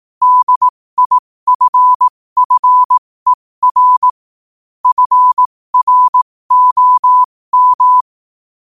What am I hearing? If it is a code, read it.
Morse code, heard as a keyed tone, about 18 wpm, spaced as for 15 wpm, 1000 Hz. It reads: DIFFER FROM
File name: qotdfiles / 15wpm